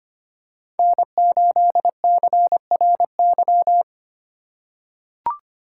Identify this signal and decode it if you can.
Morse code, heard as a keyed tone, about 25 wpm, 700 Hz. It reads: N8CRY
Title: USA Call Signs - 5 Characters - 2x Character Spacing - 25wpm